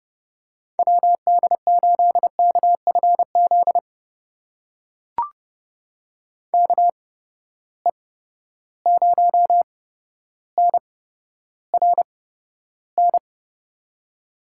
Morse code, heard as a keyed tone, about 32 wpm, spaced as for 13 wpm, 700 Hz. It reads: WD8KFZ KE0NRN
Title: USA Call Signs - 6 Characters - 8x Character Spacing - 30wpm